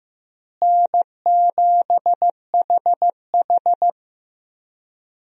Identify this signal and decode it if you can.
Morse code, heard as a keyed tone, about 15 wpm, 700 Hz. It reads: N7HH
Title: USA Call Signs - 4 Characters - Repeated 2x - 15wpm